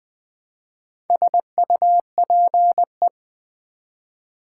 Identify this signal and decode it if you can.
Morse code, heard as a keyed tone, about 20 wpm, 700 Hz. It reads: SUPE
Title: Build A Word - 1 Letter At A Time - 3 to 5-Letter Words - 20wpm